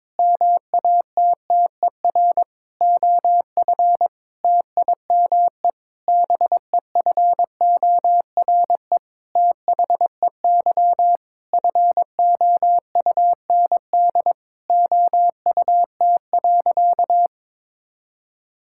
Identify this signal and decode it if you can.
Morse code, heard as a keyed tone, about 22 wpm, 700 Hz. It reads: MATTER OF TIME BEFORE THEY FOUND OUT.
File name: practice / 22wpm / Sentences with 8-Word Phrases - 22wpm